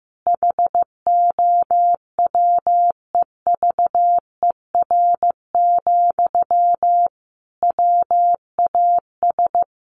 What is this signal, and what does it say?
Morse code, heard as a keyed tone, about 15 wpm, 700 Hz. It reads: HOWEVER, WAS
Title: Sentences from Top 1400 Words 15wpm